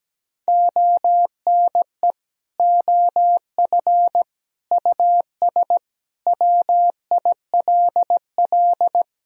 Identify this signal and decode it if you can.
Morse code, heard as a keyed tone, about 17 wpm, 700 Hz. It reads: ONE OF US WILL
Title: Sentences from Top 300 Words and Abbreviations 17wpm